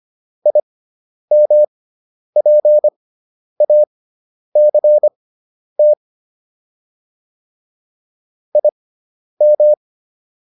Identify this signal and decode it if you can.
Morse code, heard as a keyed tone, about 25 wpm, spaced as for 10 wpm, 600 Hz. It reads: IMPACT IM